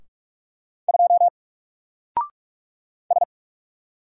Morse code, heard as a keyed tone, about 44 wpm, 700 Hz. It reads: 2 S
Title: Single Letter-Number 45wpm